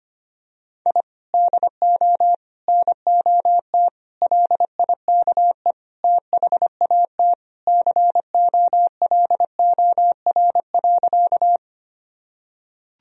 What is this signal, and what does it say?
Morse code, heard as a keyed tone, about 25 wpm, 700 Hz. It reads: I DO NOT LIKE THAT COLOR.
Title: Sentences from Top 1000 Words - Review Part 2 - 25wpm